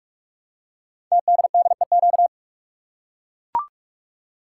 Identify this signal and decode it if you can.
Morse code, heard as a keyed tone, about 44 wpm, 700 Hz. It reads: TDDEQ
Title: Code Groups - 5 Letters - 45wpm